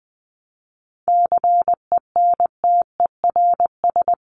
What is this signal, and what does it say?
Morse code, heard as a keyed tone, about 20 wpm, 700 Hz. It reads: CENTERS
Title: Sets of 5 Words Encore2 20wpm